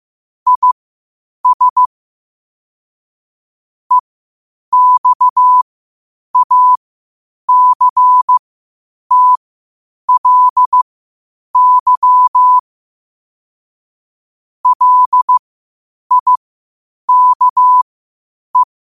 Morse code, heard as a keyed tone, about 15 wpm, spaced as for 8 wpm, 1000 Hz. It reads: IS EXACTLY LIKE
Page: Quotes for Thu, 14 Aug 2025 in Morse Code at 8 words per minute.